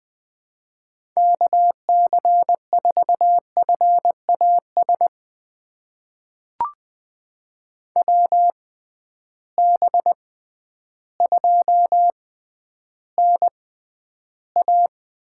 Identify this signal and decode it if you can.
Morse code, heard as a keyed tone, about 20 wpm, spaced as for 11 wpm, 700 Hz. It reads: KC4FAS WB2NA
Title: USA Call Signs - 6 Characters - 6x Character Spacing - 20wpm